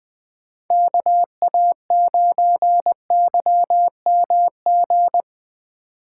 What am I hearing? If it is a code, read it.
Morse code, heard as a keyed tone, about 20 wpm, 700 Hz. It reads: KA9YMG
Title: K1USN SST QSO Practice - 20wpm